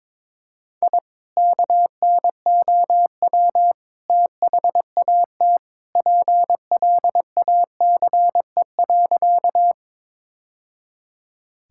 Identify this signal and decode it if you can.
Morse code, heard as a keyed tone, about 22 wpm, 700 Hz. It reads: I KNOW THAT PLACE.
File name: practice / 22wpm / Sentences with 2-Word Phrases - 22wpm